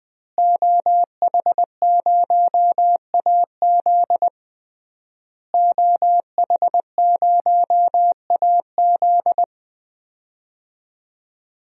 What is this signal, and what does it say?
Morse code, heard as a keyed tone, about 20 wpm, 700 Hz. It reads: OH0AZ OH0AZ
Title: DXCC-100 Call Signs - 5 Characters - Repeated 3x - 20wpm